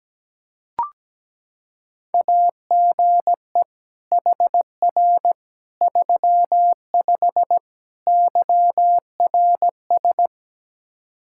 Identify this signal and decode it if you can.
Morse code, heard as a keyed tone, about 17 wpm, 700 Hz. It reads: AGE HR 35 YRS
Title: Practice QSO - Exchange 3 - More Info - 17wpm